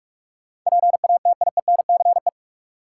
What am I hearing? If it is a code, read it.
Morse code, heard as a keyed tone, about 45 wpm, 700 Hz. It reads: PATIENCE
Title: Sentences from Top 1300 Words - New Word Drill - 45wpm